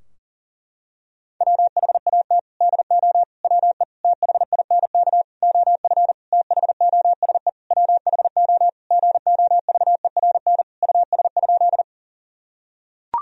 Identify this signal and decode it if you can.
Morse code, heard as a keyed tone, about 40 wpm, 700 Hz. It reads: WHAT DO WE THINK OF THOSE WHO GOVERN US?
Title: Sentences from Top 1000 Words - Review Part 2 - 40wpm